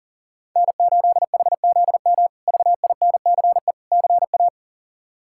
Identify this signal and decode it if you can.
Morse code, heard as a keyed tone, about 40 wpm, 700 Hz. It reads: N8HZM VINCE CA
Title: K1USN SST QSO Practice - 40wpm